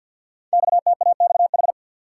Morse code, heard as a keyed tone, about 48 wpm, 700 Hz. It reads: XTAXH
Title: Code Groups - 5 Letters - 50wpm